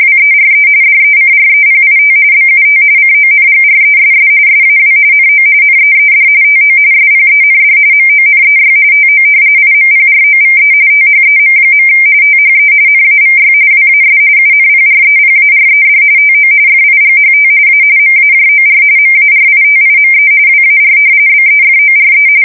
RTTY, about 45.45 baud. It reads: WERE DECREPIT BUT CONGRESS SAW NO NEED TO SPEND MONEY TO IMPROVE THEM. THE NAVY WAS UNPREPARED TO FIGHT A MAJOR MARITIME WAR BEFORE
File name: fsk170 / usnavy3